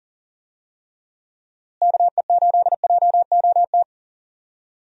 Morse code, heard as a keyed tone, about 40 wpm, 700 Hz. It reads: KE8JOT
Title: USA Call Signs - 6 Characters - Repeated 3x - 40wpm